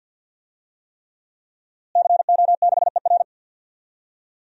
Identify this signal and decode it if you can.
Morse code, heard as a keyed tone, about 47 wpm, 700 Hz. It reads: CO6ER
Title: DXCC-100 Call Signs - 5 Characters - Repeated 3x - 50wpm